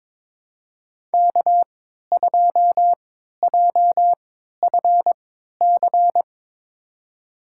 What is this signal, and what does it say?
Morse code, heard as a keyed tone, about 22 wpm, spaced as for 12 wpm, 700 Hz. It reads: K2JFC
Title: USA Call Signs - 5 Characters - 3x Character Spacing - 22wpm